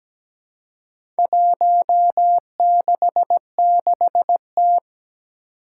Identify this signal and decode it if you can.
Morse code, heard as a keyed tone, about 17 wpm, 700 Hz. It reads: 166T
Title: Four Challenging Letter-Number Combinations 17wpm